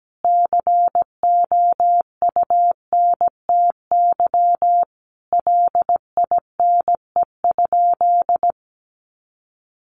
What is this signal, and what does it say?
Morse code, heard as a keyed tone, about 17 wpm, 700 Hz. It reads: COUNTY LINE?
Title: Sentences from Top 1200 Words 17wpm